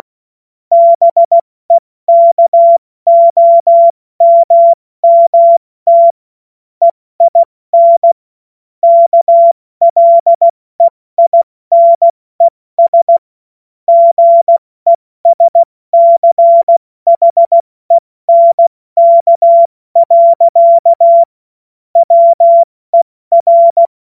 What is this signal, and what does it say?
Morse code, heard as a keyed tone, about 16 wpm, spaced as for 14 wpm, 680 Hz. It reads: BEKOMMT EIN KLEINES GESCHENK. WER
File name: Morsekurs / Witze / Witz 09. November 2021, 70 F80